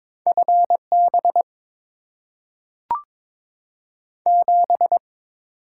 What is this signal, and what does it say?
Morse code, heard as a keyed tone, about 22 wpm, 700 Hz. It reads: FB 7
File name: Top 20 QSO Elements - 8x Character Spacing - 22wpm